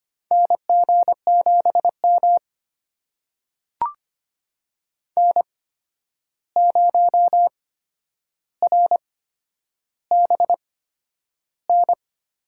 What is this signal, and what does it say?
Morse code, heard as a keyed tone, about 25 wpm, spaced as for 10 wpm, 700 Hz. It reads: NG7M N0RBN
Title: USA Call Signs - 8x Character Spacing - 25wpm